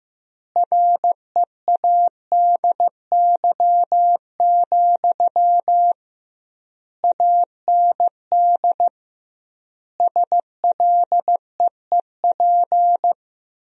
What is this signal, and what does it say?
Morse code, heard as a keyed tone, about 15 wpm, 700 Hz. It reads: READY, AND SLEEP
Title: 2x Word-Spacing - Sentences from Top 500 Words - 15wpm